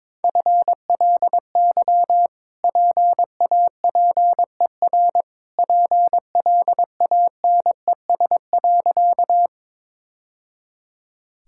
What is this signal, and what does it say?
Morse code, heard as a keyed tone, about 21 wpm, 700 Hz. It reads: FLY PAPER PLANES.
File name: practice / 22wpm / Easy Sentences with Hints - 22wpm